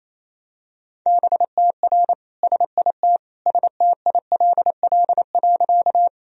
Morse code, heard as a keyed tone, about 28 wpm, 700 Hz. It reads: BTR SIT STILL.
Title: Sentences from Top 500 Words and Abbreviations 28wpm